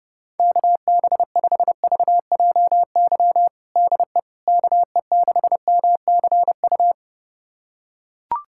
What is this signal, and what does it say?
Morse code, heard as a keyed tone, about 30 wpm, 700 Hz. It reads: KB5VJY DE KE6MCU
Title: USA Callsign DE Callsign 30wpm